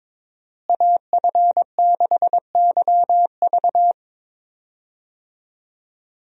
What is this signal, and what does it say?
Morse code, heard as a keyed tone, about 22 wpm, 700 Hz. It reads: AF6YV
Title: USA Call Signs - 5 Characters - Repeated 2x - 22wpm